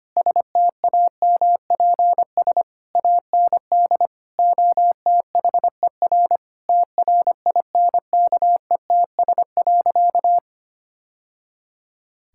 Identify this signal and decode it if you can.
Morse code, heard as a keyed tone, about 25 wpm, 700 Hz. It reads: STAMPS AND OTHER TRINKETS.
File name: practice / 25wpm / Sentences with 2-Word Phrases - 25wpm